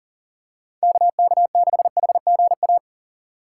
Morse code, heard as a keyed tone, about 40 wpm, 700 Hz. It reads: KK6HGA
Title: USA Call Signs - 3x Character Spacing - 40wpm